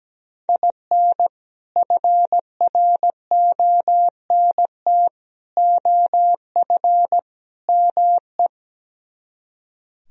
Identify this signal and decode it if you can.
Morse code, heard as a keyed tone, about 17 wpm, 700 Hz. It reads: IN FRONT OF ME
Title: Speed-Racing - Sets of 4 Words 17wpm